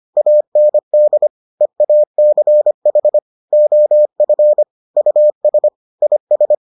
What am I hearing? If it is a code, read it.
Morse code, heard as a keyed tone, about 25 wpm, 600 Hz. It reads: AND EACH OF US IS